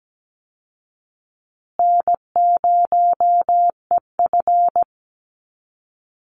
Morse code, heard as a keyed tone, about 17 wpm, 700 Hz. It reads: N0EF
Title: Easy Call Signs Repeated 3x - 17wpm